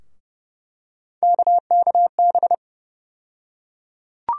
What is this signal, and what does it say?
Morse code, heard as a keyed tone, about 30 wpm, 700 Hz. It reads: KKB
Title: USA Call Sign Postfixes - 3 Letters 30wpm